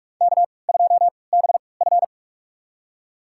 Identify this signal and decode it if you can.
Morse code, heard as a keyed tone, about 48 wpm, spaced as for 26 wpm, 700 Hz. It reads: K2BF
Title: USA Call Signs - 4 Characters - 3x Character Spacing - 45wpm